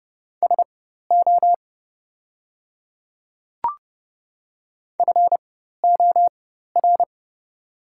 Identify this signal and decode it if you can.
Morse code, heard as a keyed tone, about 30 wpm, spaced as for 14 wpm, 700 Hz. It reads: SO FOR
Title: 4x Character Spacing - Top 100 Words - 30wpm